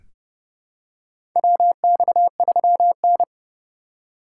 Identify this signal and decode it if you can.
Morse code, heard as a keyed tone, about 30 wpm, 700 Hz. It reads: WX3N
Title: USA Call Signs - 4 Characters - 4x Character Spacing - 30wpm